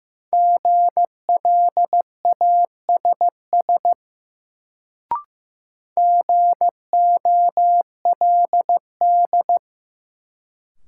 Morse code, heard as a keyed tone, about 15 wpm, 700 Hz. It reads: GLASS GOLD
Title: Sentences from Top 1000 Words - Review Part 3 - New Word Drill - 15wpm